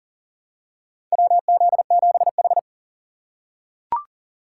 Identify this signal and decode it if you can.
Morse code, heard as a keyed tone, about 39 wpm, 700 Hz. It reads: WZ7H
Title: USA Call Signs - 4 Characters - 8x Character Spacing - 40wpm